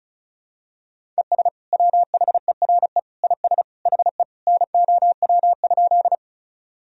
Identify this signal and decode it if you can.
Morse code, heard as a keyed tone, about 34 wpm, 700 Hz. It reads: ES WHERE IS HE NOW?
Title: Sentences from Top 200 Words and Abbreviations 35wpm